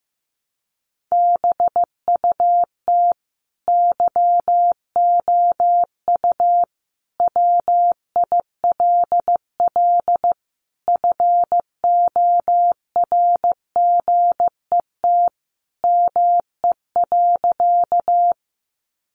Morse code, heard as a keyed tone, about 15 wpm, 700 Hz. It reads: BUT YOU WILL FORGET ME.
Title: Sentences from Top 1200 Words 15wpm